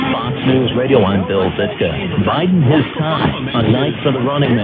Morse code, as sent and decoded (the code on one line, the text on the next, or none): none